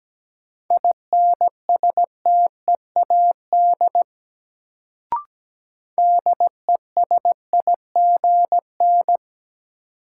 Morse code, heard as a keyed tone, about 17 wpm, 700 Hz. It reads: INSTEAD DESIGN
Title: Sentences from Top 800 Words - New Word Drill - 17wpm